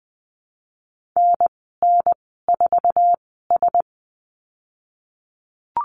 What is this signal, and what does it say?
Morse code, heard as a keyed tone, about 20 wpm, spaced as for 14 wpm, 700 Hz. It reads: NN4S
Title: 088 Lesson 15g - Number 5 - Call Signs 20wpm